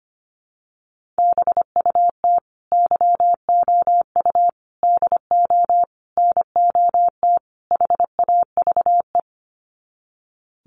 Morse code, heard as a keyed tone, about 25 wpm, 700 Hz. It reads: BUT YOU DO NOT HAVE
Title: Sets of 5 Words Encore 25wpm